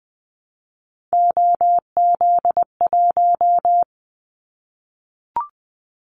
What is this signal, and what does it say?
Morse code, heard as a keyed tone, about 20 wpm, 700 Hz. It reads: OZ1
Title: Callsign Prefixes for Top 50 DXCC - 20wpm